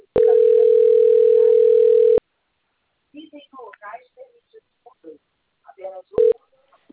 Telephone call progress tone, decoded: ringback tone

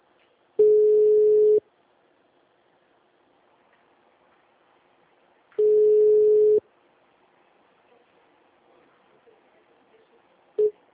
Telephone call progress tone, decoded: ringback tone

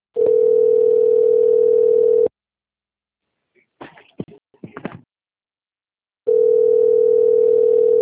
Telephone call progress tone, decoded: ringback tone

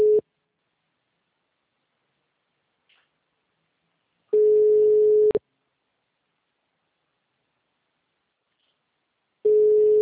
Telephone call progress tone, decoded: ringback tone

